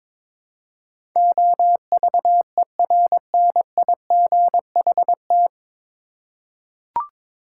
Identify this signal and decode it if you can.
Morse code, heard as a keyed tone, about 22 wpm, 700 Hz. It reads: OVERNIGHT
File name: 9 Letter Words - First 3 Letters Given - 22wpm